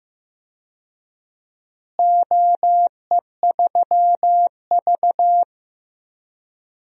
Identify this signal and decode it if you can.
Morse code, heard as a keyed tone, about 15 wpm, 700 Hz. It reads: OE3V
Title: DXCC-100 Call Signs - 4 Characters - Repeated 3x - 15wpm